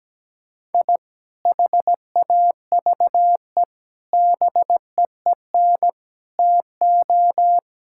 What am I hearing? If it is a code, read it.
Morse code, heard as a keyed tone, about 17 wpm, 700 Hz. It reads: I HAVE BEEN TO
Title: Sentences from Top 1000 Words - Review Part 5 - 17wpm